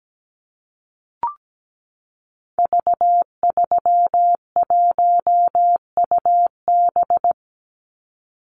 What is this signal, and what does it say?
Morse code, heard as a keyed tone, about 17 wpm, 700 Hz. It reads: V31UB
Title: Call Signs Repeated 2x - 17wpm